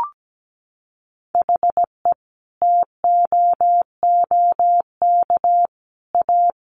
Morse code, heard as a keyed tone, about 17 wpm, 700 Hz. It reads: HE TOOK A
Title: Sentences from Top 1000 Words 17wpm